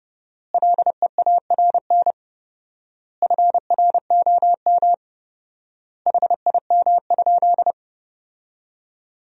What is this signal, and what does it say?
Morse code, heard as a keyed tone, about 32 wpm, 700 Hz. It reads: LEARN FROM HIM?
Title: 4x Word-Spacing - Sentences from Top 300 Words - 30wpm